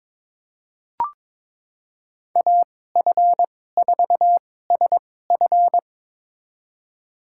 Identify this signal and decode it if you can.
Morse code, heard as a keyed tone, about 21 wpm, spaced as for 16 wpm, 700 Hz. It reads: AF4SF
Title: USA Call Signs - 2x Character Spacing - 22wpm